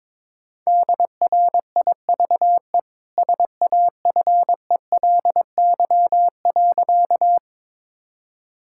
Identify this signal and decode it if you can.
Morse code, heard as a keyed tone, about 22 wpm, 690 Hz. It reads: DRIVE SAFELY.
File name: practice / 22wpm / Easy Sentences with Hints - 22wpm